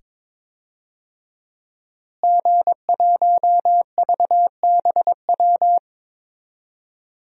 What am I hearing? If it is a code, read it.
Morse code, heard as a keyed tone, about 22 wpm, 700 Hz. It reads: G1VBW
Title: DXCC-100 Call Signs - 5 Characters - Repeated 2x - 22wpm